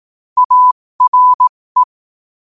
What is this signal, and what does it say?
Morse code, heard as a keyed tone, about 18 wpm, spaced as for 15 wpm, 1000 Hz. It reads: ARE
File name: qotdfiles / 15wpm